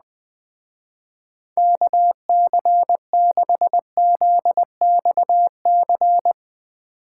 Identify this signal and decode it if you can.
Morse code, heard as a keyed tone, about 20 wpm, 700 Hz. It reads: KC6ZXC